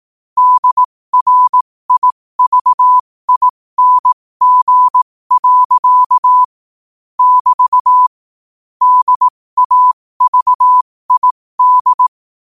Morse code, heard as a keyed tone, about 18 wpm, spaced as for 15 wpm, 1000 Hz. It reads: DRIVING. <BT> DAVID
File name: qotdfiles / 15wpm